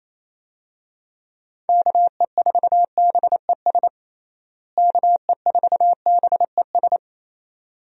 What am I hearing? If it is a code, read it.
Morse code, heard as a keyed tone, about 28 wpm, 700 Hz. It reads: KE4BES KE4BES
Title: USA Call Signs - 6 Characters - Repeated 2x - 28wpm